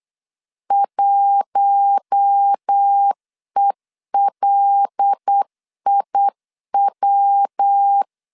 Morse code, heard as a keyed tone, about 8 wpm, 800 Hz. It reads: 1ELIW
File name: LEZIONI / 7 GF9Z / 7 GF 8wpm 3108